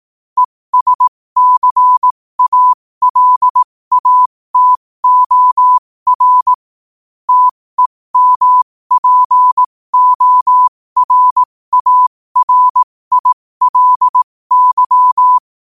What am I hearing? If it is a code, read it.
Morse code, heard as a keyed tone, about 18 wpm, spaced as for 15 wpm, 1000 Hz. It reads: ESCALATOR TEMPORARILY
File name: qotdfiles / 15wpm